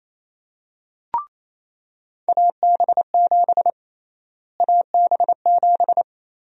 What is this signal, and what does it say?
Morse code, heard as a keyed tone, about 28 wpm, 700 Hz. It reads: AB7 AB7
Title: USA Call Sign Prefixes - Repeated 2x - 28wpm